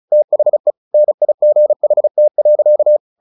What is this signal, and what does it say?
Morse code, heard as a keyed tone, about 34 wpm, 600 Hz. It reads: THE NIGHT.
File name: the old man and the sea 2 35wpm